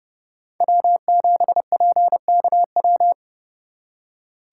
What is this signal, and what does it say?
Morse code, heard as a keyed tone, about 39 wpm, 700 Hz. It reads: W7PKW